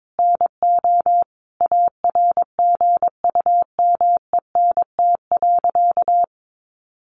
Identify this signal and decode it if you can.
Morse code, heard as a keyed tone, about 22 wpm, 700 Hz. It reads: NO ARGUMENT.
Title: A Sound Of Thunder 22wpm